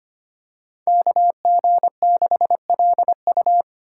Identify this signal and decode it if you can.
Morse code, heard as a keyed tone, about 25 wpm, 700 Hz. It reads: KG6LU
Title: USA Call Signs - 5 Characters - 8x Character Spacing - 25wpm